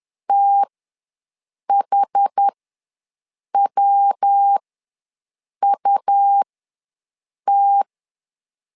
Morse code, heard as a keyed tone, about 10 wpm, spaced as for 6 wpm, 800 Hz. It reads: THWUT
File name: LEZIONI / 3 MWRO / 3 MWR ws 10wpm 2151